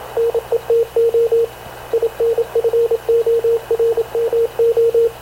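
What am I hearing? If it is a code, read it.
Morse code, heard as a keyed tone, about 27 wpm, 460 Hz, with radio noise. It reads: NETO INFORMO
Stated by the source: Havana Radio / CLA sending Spanish press (PX) in Morse, recorded by Radio Officer Paul Dumesnil onboard CNAV QUEST/CZDO.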